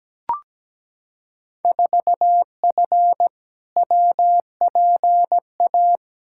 Morse code, heard as a keyed tone, about 17 wpm, 700 Hz. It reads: 4F WPA
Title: Field Day Exchanges - 17wpm